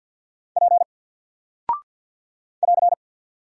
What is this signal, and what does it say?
Morse code, heard as a keyed tone, about 48 wpm, 700 Hz. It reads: P <AR>
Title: Single Character (Letter:Number:Punctuation:Prosign) Alt 50wpm